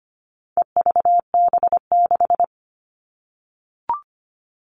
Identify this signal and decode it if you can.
Morse code, heard as a keyed tone, about 25 wpm, 700 Hz. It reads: EVB6
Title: All Cut Numbers - 4-Digit 25wpm